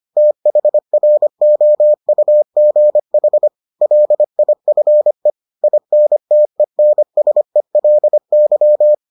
Morse code, heard as a keyed tone, about 25 wpm, 600 Hz. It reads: THROUGH LIFE INTENSELY